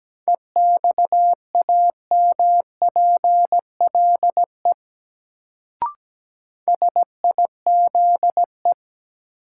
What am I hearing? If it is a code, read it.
Morse code, heard as a keyed tone, about 17 wpm, 700 Hz. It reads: EXAMPLE SIZE
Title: Sentences from Top 400 Words - New Word Drill - 17wpm